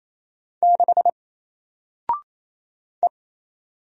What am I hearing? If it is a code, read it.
Morse code, heard as a keyed tone, about 27 wpm, 700 Hz. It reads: 6 E
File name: Single-Letter-Number-28wpm